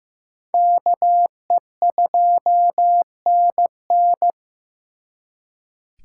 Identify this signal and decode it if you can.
Morse code, heard as a keyed tone, about 18 wpm, 700 Hz. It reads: KE2NN
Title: Speed Racing - USA Call Signs 15wpm